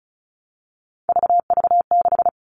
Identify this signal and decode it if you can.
Morse code, heard as a keyed tone, about 34 wpm, 700 Hz. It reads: VV6
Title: All Cut Numbers - 3-Digit 35wpm